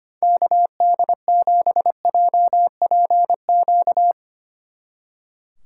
Morse code, heard as a keyed tone, about 31 wpm, 700 Hz. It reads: KD7JPQ
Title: Speed Racing - USA 2x3 Call Signs 25wpm